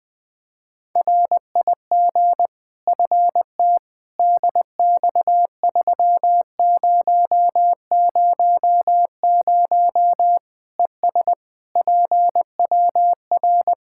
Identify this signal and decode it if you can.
Morse code, heard as a keyed tone, about 21 wpm, 700 Hz. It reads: RIG FT DX3000 ES PWR
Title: Practice QSO - Exchange 2 - Info - 20wpm